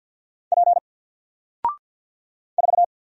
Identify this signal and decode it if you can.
Morse code, heard as a keyed tone, about 49 wpm, 700 Hz. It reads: P 4
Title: Single Character (Letter:Number:Punctuation:Prosign) 50wpm